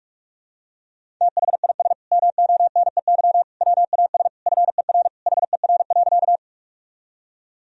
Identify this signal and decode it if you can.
Morse code, heard as a keyed tone, about 46 wpm, 700 Hz. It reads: THIS MONEY WAS FER HER.
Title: Sentences from Top 400 Words and Abbreviations 45wpm